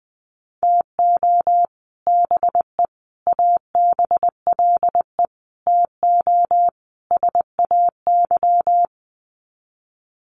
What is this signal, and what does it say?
Morse code, heard as a keyed tone, about 20 wpm, 700 Hz. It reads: TO BE ABLE TO SAY
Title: Sets of 5 Words Encore2 20wpm